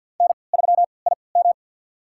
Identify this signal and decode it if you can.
Morse code, heard as a keyed tone, about 47 wpm, spaced as for 28 wpm, 690 Hz. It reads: N3IM
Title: USA Call Signs - 3x Character Spacing - 50wpm